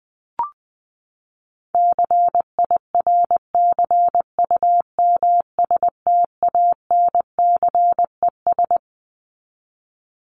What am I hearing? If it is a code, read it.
Morse code, heard as a keyed tone, about 20 wpm, 700 Hz. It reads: CIRCUMSTANCES
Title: ES:S Suffix - Top 100 Words with Base Words 20wpm